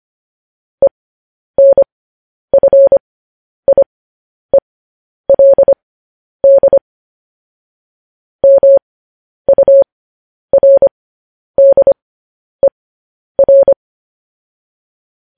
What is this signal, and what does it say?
Morse code, heard as a keyed tone, about 25 wpm, spaced as for 10 wpm, 550 Hz. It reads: ENFIELD MURDER
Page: News Headlines in Morse Code at 10 WPM